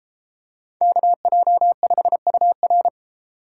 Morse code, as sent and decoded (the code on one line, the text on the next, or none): -.- .--- ..... ..- .-.
KJ5UR